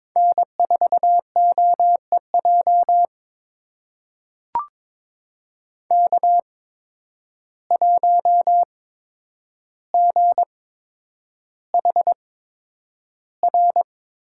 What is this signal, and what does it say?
Morse code, heard as a keyed tone, about 22 wpm, spaced as for 9 wpm, 700 Hz. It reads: N4OEJ K1GHR
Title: USA Call Signs - 5 Characters - 8x Character Spacing - 22wpm